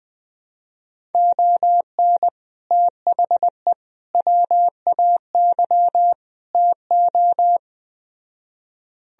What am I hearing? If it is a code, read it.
Morse code, heard as a keyed tone, about 20 wpm, 700 Hz. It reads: ON THE WAY TO